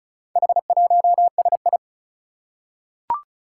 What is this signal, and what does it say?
Morse code, heard as a keyed tone, about 36 wpm, 700 Hz. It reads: H1SI
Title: Four Challenging Letter-Number Combinations 35wpm